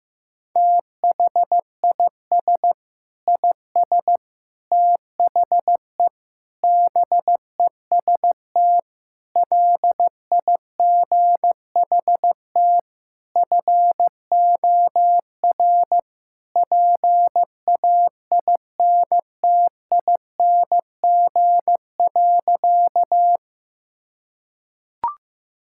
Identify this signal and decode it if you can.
Morse code, heard as a keyed tone, about 16 wpm, 700 Hz. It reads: THIS IS THE BEST LIGHT FOR PAINTING.
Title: Sentences from Top 1000 Words - Review Part 6 - 15wpm